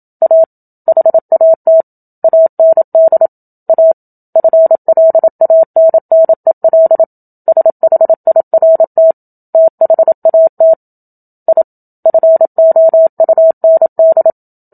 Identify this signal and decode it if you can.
Morse code, heard as a keyed tone, about 27 wpm, 650 Hz. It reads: A HAT AND A FLANNEL SHIRT THAT I FOUND
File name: Book Two / 01-Chapter 1 - 27 WPM